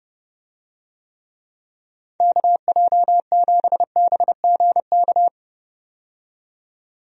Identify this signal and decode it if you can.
Morse code, heard as a keyed tone, about 29 wpm, 700 Hz. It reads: KJ7BGK
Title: USA Call Signs - 6 Characters - 30wpm